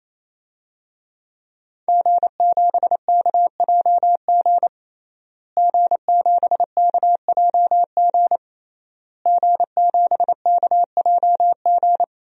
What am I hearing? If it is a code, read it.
Morse code, heard as a keyed tone, about 28 wpm, 700 Hz. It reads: G7KJG G7KJG G7KJG
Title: DXCC-100 Call Signs - 5 Characters - Repeated 3x - 28wpm